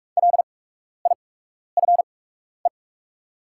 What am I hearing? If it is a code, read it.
Morse code, heard as a keyed tone, about 48 wpm, spaced as for 12 wpm, 690 Hz. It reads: LIFE
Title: 8x Character Spacing - Top 300 Words - 45wpm